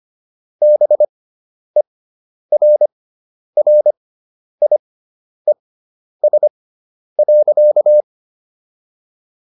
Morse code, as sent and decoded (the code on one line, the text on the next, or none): -... .-. .-. .. ... .-.-.-
BERRIES.